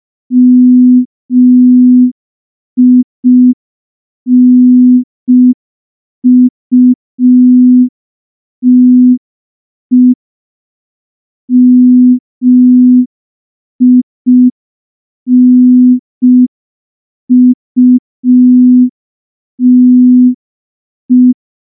Morse code, heard as a keyed tone, about 5 wpm, 250 Hz. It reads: MINUTE MINUTE
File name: morse / morse-training-05WPM